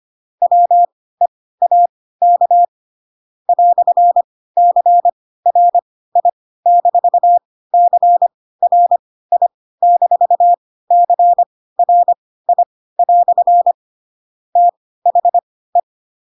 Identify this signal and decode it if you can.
Morse code, heard as a keyed tone, about 25 wpm, spaced as for 16 wpm, 700 Hz. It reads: WEAK "CRI-CRI-CRI" THE
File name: Pinocchio / Pinocchio ch4 10 25 ik1hge